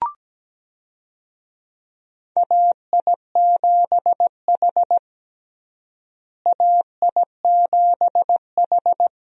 Morse code, heard as a keyed tone, about 17 wpm, 700 Hz. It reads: AI7H AI7H
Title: USA Call Signs - 4 Characters - Repeated 2x - 17wpm